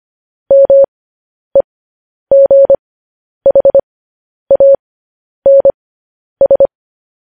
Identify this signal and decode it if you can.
Morse code, heard as a keyed tone, about 25 wpm, spaced as for 10 wpm, 550 Hz. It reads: MEGHANS